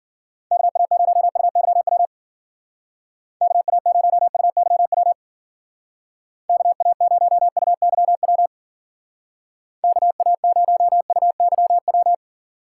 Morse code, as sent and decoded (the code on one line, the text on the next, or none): -.- .- ----- ..- -.-- .-- / -.- .- ----- ..- -.-- .-- / -.- .- ----- ..- -.-- .-- / -.- .- ----- ..- -.-- .--
KA0UYW KA0UYW KA0UYW KA0UYW